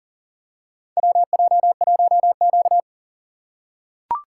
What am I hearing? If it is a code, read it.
Morse code, heard as a keyed tone, about 40 wpm, 700 Hz. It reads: WJ1Q
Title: USA Call Signs - 4x Character Spacing - 40wpm